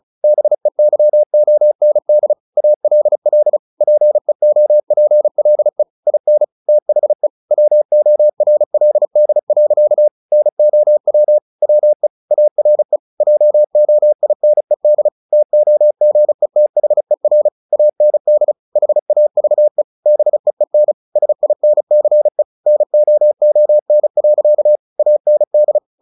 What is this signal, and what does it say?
Morse code, heard as a keyed tone, about 34 wpm, 600 Hz. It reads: BEYOND ALL PEOPLE IN THE WORLD. NOW WE ARE JOINED TOGETHER AND HAVE BEEN SINCE NOON. AND
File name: the old man and the sea / the old man and the sea 1 35wpm